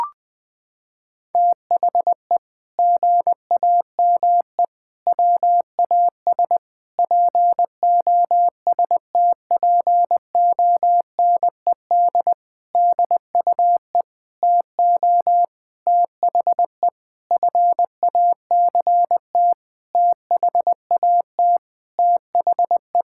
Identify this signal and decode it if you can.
Morse code, heard as a keyed tone, about 20 wpm, 700 Hz. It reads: THE GAME WAS POSTPONED DUE TO THE FACT THAT THE
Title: Sentences with 6-Word Phrases - 20wpm